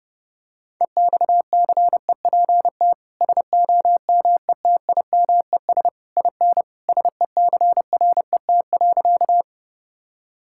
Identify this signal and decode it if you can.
Morse code, heard as a keyed tone, about 30 wpm, 700 Hz. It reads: EXCEPT SOMETIMES IN SECRET.
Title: Sentences from Top 1600 Words 30wpm